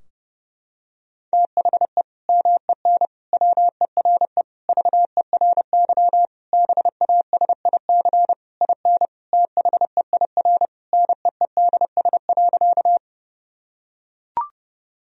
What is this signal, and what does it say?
Morse code, heard as a keyed tone, about 30 wpm, 700 Hz. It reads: THE MEN WERE VERY BASIC IN THEIR NEEDS.
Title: Sentences from Top 1000 Words - Review Part 3 - 30wpm